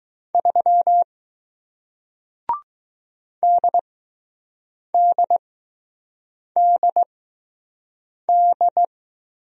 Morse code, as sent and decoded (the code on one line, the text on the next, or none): ...-- / -.. -.. -.. -..
3 DDDD